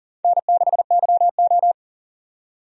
Morse code, heard as a keyed tone, about 40 wpm, 700 Hz. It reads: N6YO
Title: USA Call Signs - 4 Characters - 40wpm